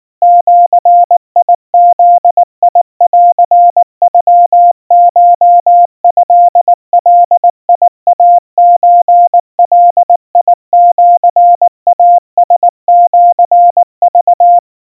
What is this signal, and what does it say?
Morse code (Japanese Wabun code), heard as a keyed tone, about 19 wpm, 700 Hz. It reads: ジブンノコトガイソガシイラシク
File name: kawa / Morse sample / kamyu 70 cpm 700 Hz Crispy